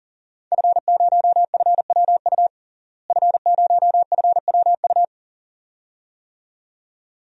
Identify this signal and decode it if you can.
Morse code, heard as a keyed tone, about 40 wpm, 700 Hz. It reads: F0FWU F0FWU
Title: DXCC-100 Call Signs - 5 Characters - Repeated 2x - 40wpm